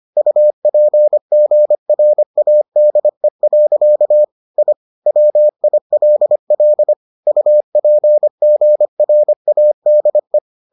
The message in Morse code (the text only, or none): UPGRADE. I WILL UPGRADE